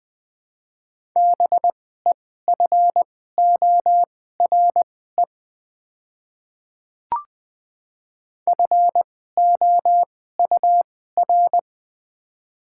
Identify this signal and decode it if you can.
Morse code, heard as a keyed tone, about 20 wpm, spaced as for 14 wpm, 700 Hz. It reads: BEFORE FOUR
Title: 2x Character Spacing - Top 500 Words - 20wpm